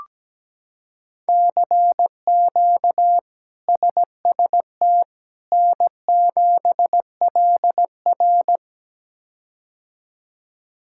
Morse code, heard as a keyed tone, about 17 wpm, 700 Hz. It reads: CQ SST N7LR
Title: K1USN SST QSO Practice - 17wpm